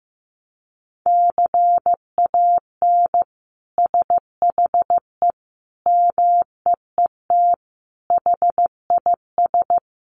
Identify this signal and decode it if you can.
Morse code, heard as a keyed tone, about 15 wpm, 700 Hz. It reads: CAN SHE MEET HIS
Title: Sentences from Top 1100 Words 15wpm